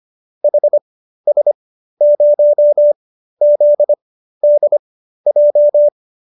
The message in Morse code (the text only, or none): HS0ZDJ